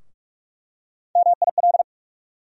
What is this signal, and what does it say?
Morse code, heard as a keyed tone, about 47 wpm, 700 Hz. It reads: MIL
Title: USA Call Sign Postfixes - 3 Letters 45wpm